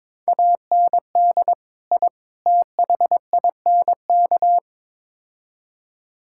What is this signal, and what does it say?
Morse code, heard as a keyed tone, about 22 wpm, 700 Hz. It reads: AND I THINK
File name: practice / 22wpm / Build A Phrase - 5-Word Phrases - 22wpm